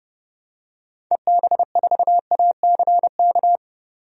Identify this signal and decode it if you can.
Morse code, heard as a keyed tone, about 30 wpm, 700 Hz. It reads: EB4ACK
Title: DXCC-100 Call Signs - 6 Characters - 30wpm